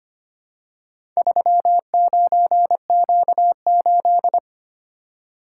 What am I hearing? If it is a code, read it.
Morse code, heard as a keyed tone, about 25 wpm, 700 Hz. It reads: 39Q8
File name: practice / 25wpm / Four Letter-Number Combinations - 25wpm